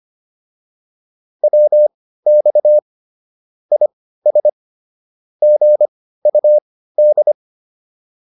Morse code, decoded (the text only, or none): WX IS GUD